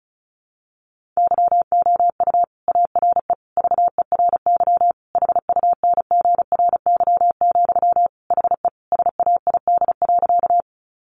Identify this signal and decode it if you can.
Morse code, heard as a keyed tone, about 34 wpm, 700 Hz. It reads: YOU ARE VERY HUNGRY, HE SAID.
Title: Sentences from Top 1400 Words 35wpm